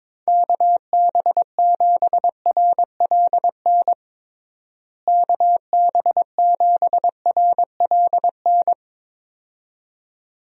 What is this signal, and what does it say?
Morse code, heard as a keyed tone, about 22 wpm, 700 Hz. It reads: KB7RLN KB7RLN
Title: USA Call Signs - 6 Characters - Repeated 2x - 22wpm